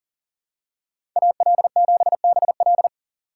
Morse code, heard as a keyed tone, about 40 wpm, 700 Hz. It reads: AL7BL
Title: USA Call Signs - 5 Characters - 4x Character Spacing - 40wpm